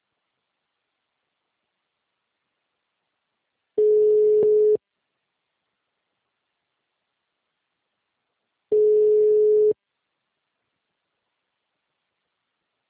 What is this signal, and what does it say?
Ringback tone (Europe, 425 Hz).
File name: out- -2050- - - .18133